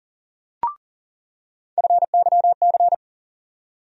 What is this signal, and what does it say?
Morse code, heard as a keyed tone, about 40 wpm, 700 Hz. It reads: FYC
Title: USA Call Sign Postfixes - 3 Letters 40wpm